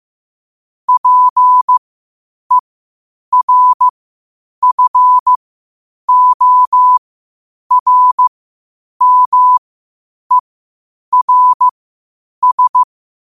Morse code, heard as a keyed tone, about 15 wpm, spaced as for 8 wpm, 1000 Hz. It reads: PERFORMERS
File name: qotdfiles / 8wpm